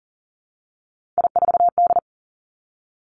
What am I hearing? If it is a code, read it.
Morse code, heard as a keyed tone, about 42 wpm, 700 Hz. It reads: I4D
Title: Three Challenging Letter-Number Combinations 40wpm